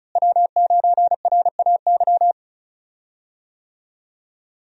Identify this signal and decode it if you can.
Morse code, heard as a keyed tone, about 34 wpm, 700 Hz. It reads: W9RAY
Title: USA Call Signs - 5 Characters - Repeated 2x - 35wpm